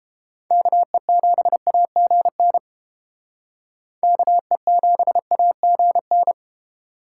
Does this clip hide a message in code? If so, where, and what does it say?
Yes: Morse code from 0.5 to 6.3 s, heard as a keyed tone, about 29 wpm, 700 Hz. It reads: KE7AGN KE7AGN